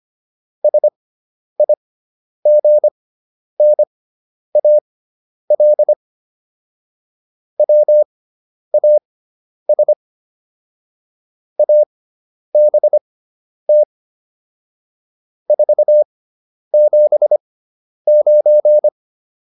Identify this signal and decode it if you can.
Morse code, heard as a keyed tone, about 25 wpm, spaced as for 10 wpm, 600 Hz. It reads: SIGNAL WAS ABT 479